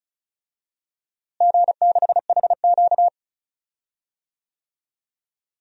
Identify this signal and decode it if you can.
Morse code, heard as a keyed tone, about 36 wpm, 700 Hz. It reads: G6HQ
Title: DXCC-100 Call Signs - 4 Characters - Repeated 2x - 35wpm